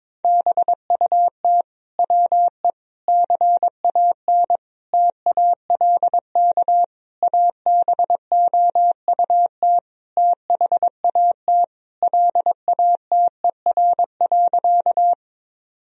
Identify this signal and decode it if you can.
Morse code, heard as a keyed tone, about 22 wpm, 700 Hz. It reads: BUT WE CAN TALK ABOUT THAT LATER.
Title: Sentences from Top 1000 Words - Review Part 4 - 22wpm